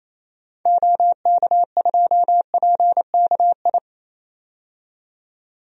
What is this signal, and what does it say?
Morse code, heard as a keyed tone, about 28 wpm, 700 Hz. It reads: OK2PKI
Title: DXCC-100 Call Signs - 6 Characters - 28wpm